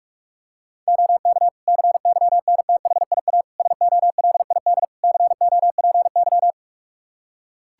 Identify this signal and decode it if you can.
Morse code, heard as a keyed tone, about 44 wpm, 700 Hz. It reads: OK CYNTHIA SOLID COPY
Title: Practice QSO - Exchange 3 - More Info - 45wpm